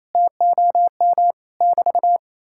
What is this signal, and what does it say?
Morse code, heard as a keyed tone, about 36 wpm, 700 Hz. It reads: TOM <BT>